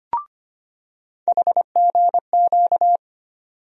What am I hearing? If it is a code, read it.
Morse code, heard as a keyed tone, about 25 wpm, 700 Hz. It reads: HGQ
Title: USA Call Sign Postfixes - 3 Letters 25wpm